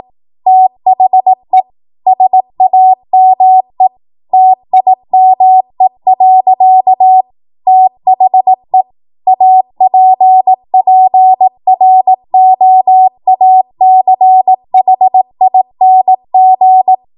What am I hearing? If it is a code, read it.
Morse code, heard as a keyed tone, about 18 wpm, 750 Hz. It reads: THE SAME TIME. THE APPROACHING
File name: Morse / 18WPM